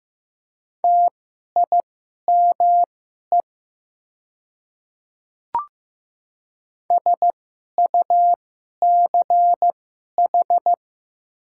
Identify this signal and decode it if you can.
Morse code, heard as a keyed tone, about 15 wpm, spaced as for 11 wpm, 700 Hz. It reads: TIME SUCH
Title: 2x Character Spacing - Top 200 Words - 15wpm